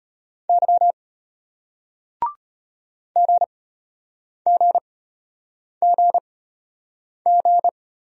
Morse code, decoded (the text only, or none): Y GGGG